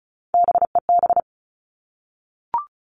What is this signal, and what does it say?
Morse code, heard as a keyed tone, about 36 wpm, 700 Hz. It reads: BEB
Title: Three Challenging Letter-Number Combinations 35wpm